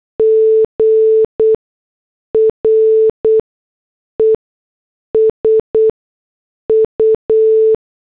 Morse code, heard as a keyed tone, about 8 wpm, spaced as for 6 wpm, 440 Hz. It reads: G RESU
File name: morse / headlines